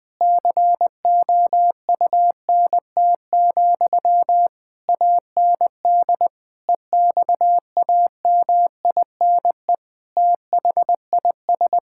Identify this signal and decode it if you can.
Morse code, heard as a keyed tone, about 20 wpm, 700 Hz. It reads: COUNT, AND EXAMINE THIS
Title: Sentences from Top 1600 Words 20wpm